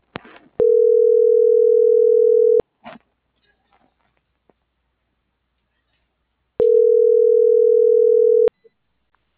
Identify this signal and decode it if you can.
Ringback tone (North America), 440 + 480 Hz.